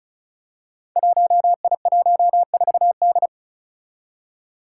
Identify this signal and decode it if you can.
Morse code, heard as a keyed tone, about 36 wpm, 700 Hz. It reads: 1I14D
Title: Five Challenging Letter-Number Combinations 35wpm